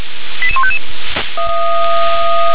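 Five-tone selective call (ZVEI), heard as quick runs of five tones, tones of about 70 ms. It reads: 88155
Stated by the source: ZVEI five-tone selective call received on SDR.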